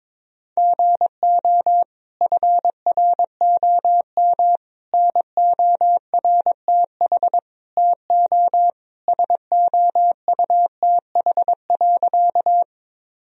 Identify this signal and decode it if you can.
Morse code, heard as a keyed tone, about 21 wpm, 700 Hz. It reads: GO FROM NORTH TO SOUTH.
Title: Sentences from Top 1000 Words - Review Part 2 - 22wpm